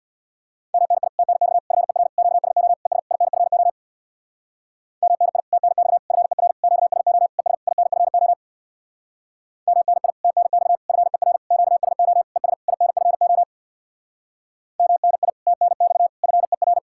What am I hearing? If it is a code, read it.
Morse code, heard as a keyed tone, about 67 wpm, 700 Hz. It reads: MNI TNX FER QSO ES INFO MNI TNX FER QSO ES INFO MNI TNX FER QSO ES INFO MNI TNX FER